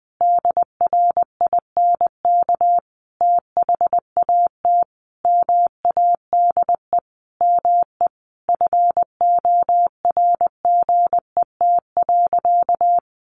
Morse code, heard as a keyed tone, about 20 wpm, 700 Hz. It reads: DRINK THAT MADE ME FORGET.
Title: Sentences from Top 1500 Words 20wpm